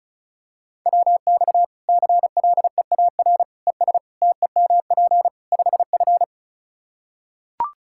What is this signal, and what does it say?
Morse code, heard as a keyed tone, about 36 wpm, 700 Hz. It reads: WX CLEAR ES TEMP 5F
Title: Practice QSO - Exchange 2 - Info - 35wpm